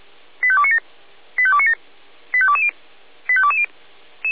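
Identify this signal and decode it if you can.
Five-tone selective call (ZVEI), heard as quick runs of five tones, tones of about 70 ms. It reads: 85297 85297 85220 85220
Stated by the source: ZVEI five-tone selective call 85211, received via SDR.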